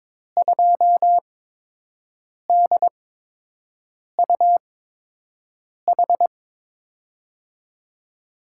Morse code, heard as a keyed tone, about 22 wpm, spaced as for 6 wpm, 700 Hz. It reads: 2DUH
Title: USA Call Signs - 5 Characters - 8x Character Spacing - 22wpm